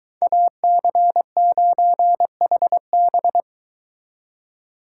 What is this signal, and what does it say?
Morse code, heard as a keyed tone, about 22 wpm, 700 Hz. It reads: AC9HB